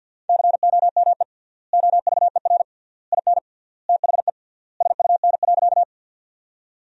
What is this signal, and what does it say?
Morse code, heard as a keyed tone, about 52 wpm, 700 Hz. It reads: COME OVER IN THE SUN.
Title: 3x Word-Spacing - Sentences from Top 300 Words - 50wpm